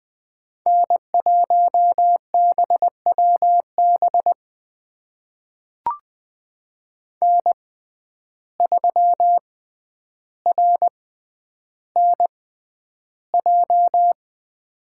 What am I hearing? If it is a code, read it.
Morse code, heard as a keyed tone, about 20 wpm, spaced as for 10 wpm, 700 Hz. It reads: N1BWB N3RNJ
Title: USA Call Signs - 5 Characters - 6x Character Spacing - 20wpm